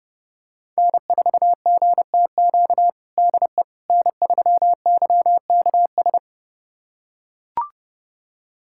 Morse code, heard as a keyed tone, about 30 wpm, 700 Hz. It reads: N4GTQ DE N3YKS
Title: USA Callsign DE Callsign - 5 Characters - 30wpm